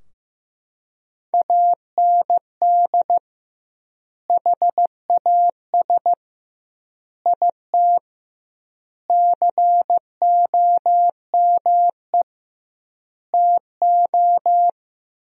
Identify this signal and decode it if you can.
Morse code, heard as a keyed tone, about 16 wpm, 700 Hz. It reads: AND HAS IT COME TO
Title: 2x Word-Spacing - Sentences from Top 100 Words - 15wpm